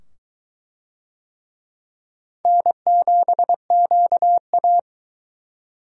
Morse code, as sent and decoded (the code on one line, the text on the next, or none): -. --... --.- .-
N7QA